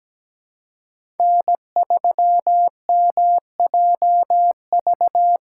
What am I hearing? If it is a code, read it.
Morse code, heard as a keyed tone, about 17 wpm, 700 Hz. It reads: N3MJV
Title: USA Call Signs - 2x Character Spacing - 17wpm